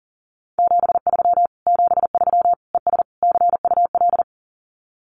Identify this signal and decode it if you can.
Morse code, heard as a keyed tone, about 40 wpm, 700 Hz. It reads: 73 73 ES CUL
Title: Common QSO Phrases Encore 40wpm